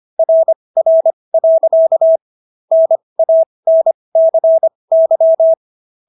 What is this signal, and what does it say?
Morse code, heard as a keyed tone, about 25 wpm, spaced as for 20 wpm, 650 Hz. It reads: RR. NANCY